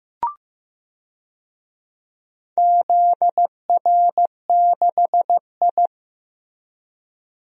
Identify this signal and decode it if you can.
Morse code, heard as a keyed tone, about 15 wpm, 700 Hz. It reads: ZR6I
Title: DXCC-100 Call Signs - 4 Characters - Repeated 3x - 15wpm